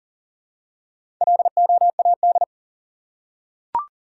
Morse code, heard as a keyed tone, about 40 wpm, 700 Hz. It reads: LOAD
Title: Four Letter Words - 40wpm